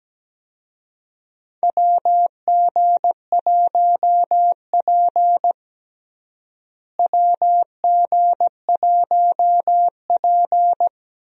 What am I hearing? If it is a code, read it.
Morse code, heard as a keyed tone, about 17 wpm, 700 Hz. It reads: WG1P WG1P
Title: USA Call Signs - 4 Characters - Repeated 2x - 17wpm